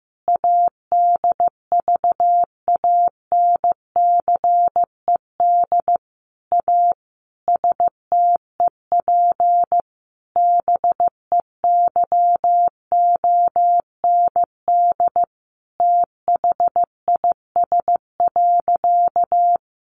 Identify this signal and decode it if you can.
Morse code, heard as a keyed tone, about 15 wpm, 700 Hz. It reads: ADVANCED A STEP BEYOND THIS.
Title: Sentences from Top 1200 Words 15wpm